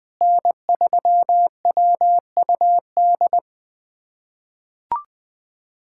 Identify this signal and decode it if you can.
Morse code, heard as a keyed tone, about 20 wpm, 700 Hz. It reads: N3WUD
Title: USA Callsign DE Callsign - 5 Characters - 20wpm